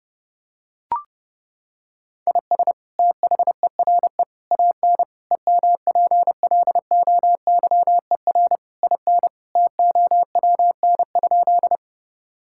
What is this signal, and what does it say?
Morse code, heard as a keyed tone, about 30 wpm, 700 Hz. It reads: IS THERE AN EMPLOYER IN TOWN?
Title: Sentences from Top 1600 Words 30wpm